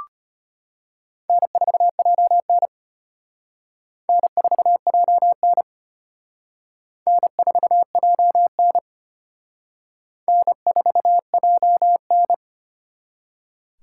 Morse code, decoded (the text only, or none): N4JN N4JN N4JN N4JN